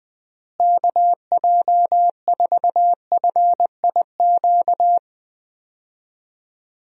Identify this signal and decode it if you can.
Morse code, heard as a keyed tone, about 21 wpm, 700 Hz. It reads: KJ4FIQ
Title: USA Call Signs - 6 Characters - Repeated 3x - 20wpm